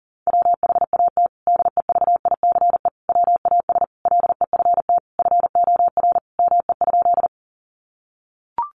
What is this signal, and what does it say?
Morse code, heard as a keyed tone, about 40 wpm, 700 Hz. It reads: WHAT DEVICE WAS LEFT FOR ME?
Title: Sentences from Top 1500 Words 40wpm